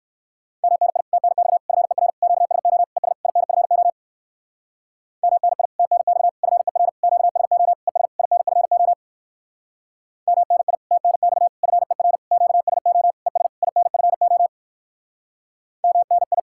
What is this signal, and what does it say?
Morse code, heard as a keyed tone, about 59 wpm, 700 Hz. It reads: MNI TNX FER QSO ES INFO MNI TNX FER QSO ES INFO MNI TNX FER QSO ES INFO MNI